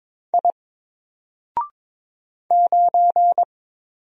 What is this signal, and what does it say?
Morse code, heard as a keyed tone, about 22 wpm, 700 Hz. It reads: I 9
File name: Single-Letter-Number-22wpm